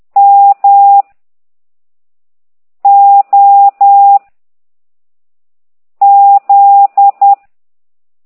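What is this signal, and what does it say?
Morse code, heard as a keyed tone, about 10 wpm, 800 Hz. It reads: M O Z
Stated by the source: Morse code lesson 5 exercise 10 by CT2IDW, containing only characters M, O, G, Z, Q and comma.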